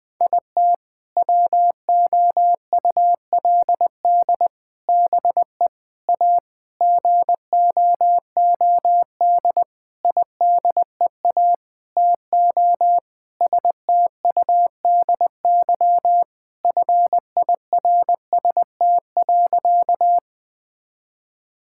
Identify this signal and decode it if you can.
Morse code, heard as a keyed tone, about 20 wpm, 700 Hz. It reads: IT WOULD BE A GOOD IDEA TO STUDY FIRST.
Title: Sentences with 7-Word Phrases - Spoken Beginning - 20wpm